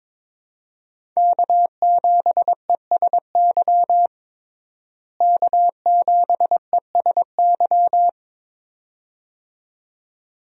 Morse code, heard as a keyed tone, about 22 wpm, 700 Hz. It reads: K7ESY K7ESY
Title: USA Call Signs - 5 Characters - Repeated 2x - 22wpm